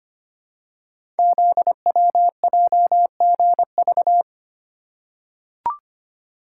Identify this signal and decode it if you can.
Morse code, heard as a keyed tone, about 25 wpm, 700 Hz. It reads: ZWJGV
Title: Code Groups - 5 Letters - 25wpm